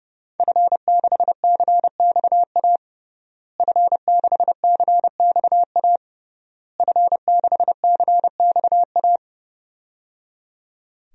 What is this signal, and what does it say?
Morse code, heard as a keyed tone, about 29 wpm, 700 Hz. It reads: F6CXA F6CXA F6CXA
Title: DXCC-100 Call Signs - 5 Characters - Repeated 3x - 30wpm